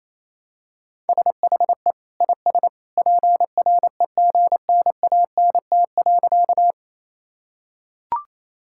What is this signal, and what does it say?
Morse code, heard as a keyed tone, about 27 wpm, 700 Hz. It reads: SHE IS PREGNANT.
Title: Sentences from Top 1000 Words - Review Part 6 - 28wpm